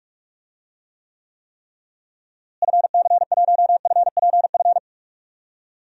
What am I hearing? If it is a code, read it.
Morse code, heard as a keyed tone, about 46 wpm, 700 Hz. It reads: FC1FPF
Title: DXCC-100 Call Signs - 6 Characters - Repeated 3x - 45wpm